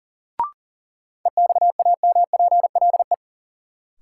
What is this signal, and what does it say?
Morse code, heard as a keyed tone, about 40 wpm, 700 Hz. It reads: EXAMPLE
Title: Sets of 2 Words - New Word Drill - 40wpm